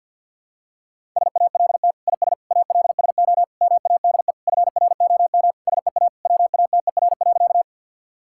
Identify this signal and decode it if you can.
Morse code, heard as a keyed tone, about 51 wpm, 700 Hz. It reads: SALT IS ALSO MADE FROM SEA WATER.
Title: Sentences from Top 1000 Words - Review Part 5 - 50wpm